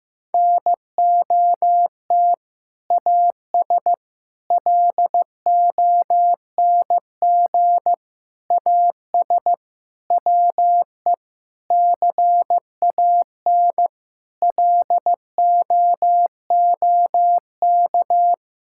Morse code, heard as a keyed tone, about 15 wpm, 700 Hz. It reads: NOT AS LONG AS WE CAN LOOK
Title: Sentences from Top 1000 Words - Review Part 6 - 15wpm